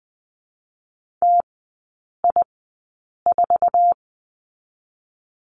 Morse code, heard as a keyed tone, about 20 wpm, 700 Hz. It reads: T I 4
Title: 047 Lesson 9e - Number 4 - Three Characters 20wpm